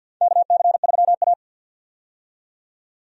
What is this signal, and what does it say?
Morse code, heard as a keyed tone, about 51 wpm, 700 Hz. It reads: KC3A
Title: USA Call Signs - 4 Characters - 50wpm